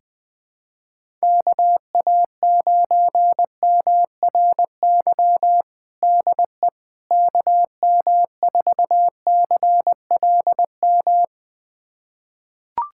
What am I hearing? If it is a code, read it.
Morse code, heard as a keyed tone, about 21 wpm, 700 Hz. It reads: KA9MRY DE KM4CLM
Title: USA Callsign DE Callsign - 6 Characters - 20wpm